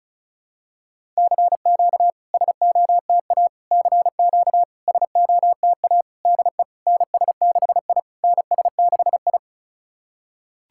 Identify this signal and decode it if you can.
Morse code, heard as a keyed tone, about 36 wpm, 690 Hz. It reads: CQ SOTA CQ SOTA DE NS6I NS6I
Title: SOTA QSO Practice 35wpm